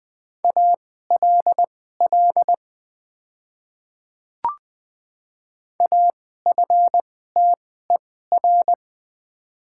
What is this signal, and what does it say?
Morse code, heard as a keyed tone, about 20 wpm, spaced as for 14 wpm, 700 Hz. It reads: ALL AFTER
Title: 2x Character Spacing - Top 200 Words - 20wpm